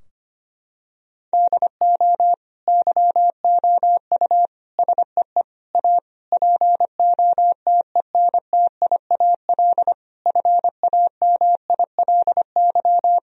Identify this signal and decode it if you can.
Morse code, heard as a keyed tone, about 25 wpm, 700 Hz. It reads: DO YOU SEE A POTENTIAL FAMILY
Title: Sentences from Top 1000 Words - Review Part 4 - 25wpm